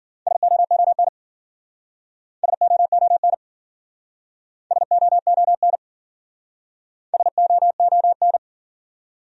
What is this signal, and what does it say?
Morse code, heard as a keyed tone, about 52 wpm, 700 Hz. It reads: SOON SOON SOON SOON